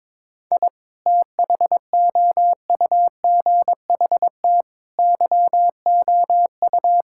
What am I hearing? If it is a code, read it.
Morse code, heard as a keyed tone, about 22 wpm, 700 Hz. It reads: I THOUGHT YOU
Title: Sentences from Top 1000 Words - Review Part 5 - 22wpm